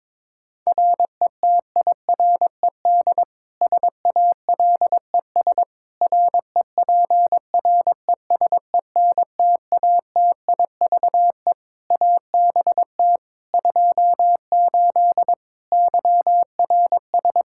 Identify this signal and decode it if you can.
Morse code, heard as a keyed tone, about 22 wpm, 700 Hz. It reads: RETIRED SALES REPRESENTATIVE ABT 28 YRS
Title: Practice QSO - Exchange 3 - More Info - 22wpm